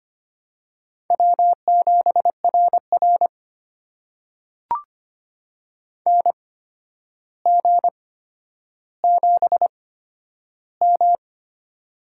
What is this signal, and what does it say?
Morse code, heard as a keyed tone, about 25 wpm, spaced as for 11 wpm, 700 Hz. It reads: W7RR NG7M
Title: USA Call Signs - 4 Characters - 8x Character Spacing - 25wpm